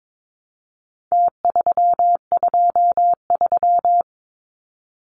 Morse code, heard as a keyed tone, about 22 wpm, 700 Hz. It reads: T323
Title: Numbers - Contest Style - 4-Digit 22wpm